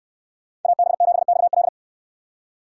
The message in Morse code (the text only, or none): AB7XL